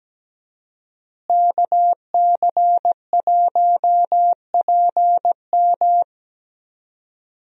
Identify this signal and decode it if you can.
Morse code, heard as a keyed tone, about 17 wpm, 700 Hz. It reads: KC1PM
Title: USA Call Signs - 5 Characters - Repeated 2x - 17wpm